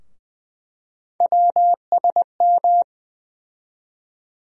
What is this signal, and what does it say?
Morse code, heard as a keyed tone, about 20 wpm, 700 Hz. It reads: WSM
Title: USA Call Sign Postfixes - 3 Letters 20wpm